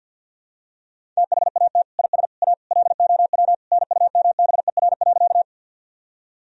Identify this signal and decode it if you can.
Morse code, heard as a keyed tone, about 46 wpm, 700 Hz. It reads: THAT IS A LOW NUMBER.
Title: Sentences from Top 1000 Words - Review Part 1 - 50wpm